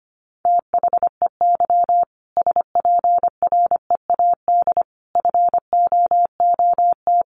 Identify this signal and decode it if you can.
Morse code, heard as a keyed tone, about 25 wpm, 700 Hz. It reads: THEY SPREAD FOOT
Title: Sentences from Top 1500 Words 25wpm